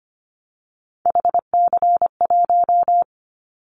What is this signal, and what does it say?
Morse code, heard as a keyed tone, about 25 wpm, 700 Hz. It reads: HC1
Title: Callsign Prefixes for Top 100 DXCC - 25wpm